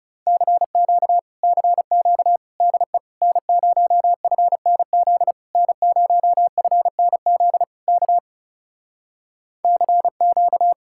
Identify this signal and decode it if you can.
Morse code, heard as a keyed tone, about 40 wpm, 700 Hz. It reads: CQ CQ DE N0FNZ N0FNZ K CQ